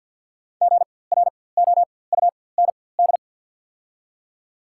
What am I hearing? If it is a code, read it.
Morse code, heard as a keyed tone, about 54 wpm, spaced as for 24 wpm, 690 Hz. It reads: GROUND
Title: 4x Character Spacing - Top 500 Words - 50wpm